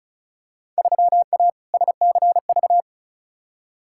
Morse code, heard as a keyed tone, about 36 wpm, 700 Hz. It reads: 3A SCV
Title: Field Day Exchanges - 35wpm